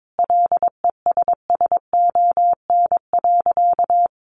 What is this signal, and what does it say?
Morse code, heard as a keyed tone, about 22 wpm, 700 Hz. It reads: LESSON.
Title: Sentences from Top 1200 Words 22wpm